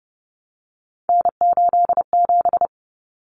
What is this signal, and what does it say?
Morse code, heard as a keyed tone, about 30 wpm, 700 Hz. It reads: N87
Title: Common Cut Numbers - 3-Digit 30wpm